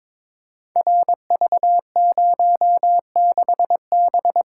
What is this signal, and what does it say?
Morse code, heard as a keyed tone, about 21 wpm, 700 Hz. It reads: RV06B
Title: Code Groups - 5 Letter:Numbers - 22wpm